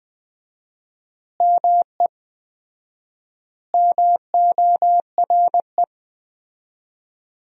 Morse code, heard as a keyed tone, about 20 wpm, 700 Hz. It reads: ME MORE
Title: 4x Word-Spacing - Sentences from Top 200 Words - 20wpm